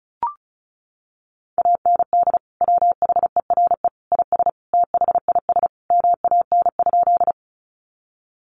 Morse code, heard as a keyed tone, about 34 wpm, 700 Hz. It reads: AND WHERE IS THIS MAN?
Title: Sentences from Top 200 Words 35wpm